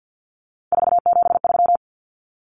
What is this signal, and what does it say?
Morse code, heard as a keyed tone, about 50 wpm, 700 Hz. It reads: 473
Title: Numbers - 3-Digit 50wpm